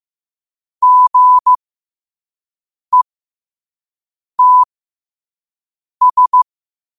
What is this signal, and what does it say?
Morse code, heard as a keyed tone, about 15 wpm, spaced as for 5 wpm, 1000 Hz. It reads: GETS
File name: qotdfiles / 5wpm